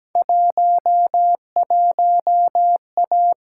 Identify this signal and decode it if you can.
Morse code, heard as a keyed tone, about 17 wpm, 700 Hz. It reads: 11A
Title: Field Day Exchanges Encore - 17wpm